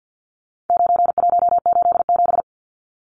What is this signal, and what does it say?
Morse code, heard as a keyed tone, about 50 wpm, 700 Hz. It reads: 9187
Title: Numbers - 4-Digit 50wpm